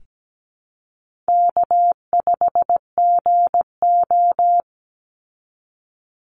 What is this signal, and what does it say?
Morse code, heard as a keyed tone, about 17 wpm, 700 Hz. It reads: K5GO
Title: Easy Call Signs Repeated 2x - 17wpm